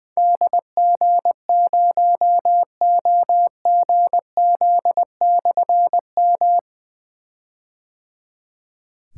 Morse code, heard as a keyed tone, about 20 wpm, 700 Hz. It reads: DG0OGZ/M
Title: Difficult Call Signs 20wpm